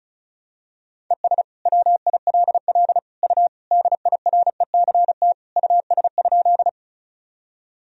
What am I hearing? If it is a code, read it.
Morse code, heard as a keyed tone, about 36 wpm, 700 Hz. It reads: ES WILL U DIRECT US?
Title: Sentences from Top 400 Words and Abbreviations 35wpm